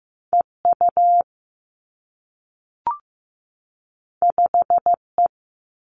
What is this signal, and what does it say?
Morse code, heard as a keyed tone, about 15 wpm, 700 Hz. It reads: E U 5 E
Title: Two Challenging Letter-Number Combinations 15wpm